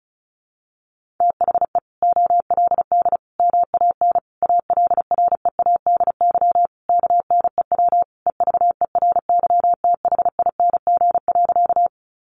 Morse code, heard as a keyed tone, about 34 wpm, 700 Hz. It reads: THE OLD MAN ALREADY KNEW EVERYTHING.
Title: Sentences from Top 700 Words 35wpm